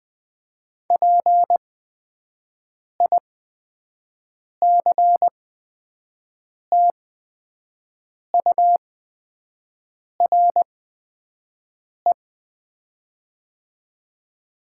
Morse code, heard as a keyed tone, about 20 wpm, spaced as for 5 wpm, 700 Hz. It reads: PICTURE
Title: 8x Character Spacing - Top 300 Words - 20wpm